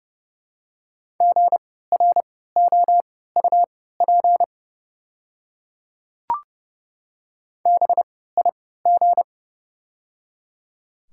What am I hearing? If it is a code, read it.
Morse code, heard as a keyed tone, about 30 wpm, spaced as for 17 wpm, 700 Hz. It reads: GROUP BIG
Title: 3x Character Spacing - Top 300 Words - 30wpm